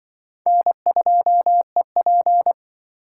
Morse code, heard as a keyed tone, about 25 wpm, 700 Hz. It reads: N2EP
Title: Speed Racing - USA 1x2 Call Signs 20wpm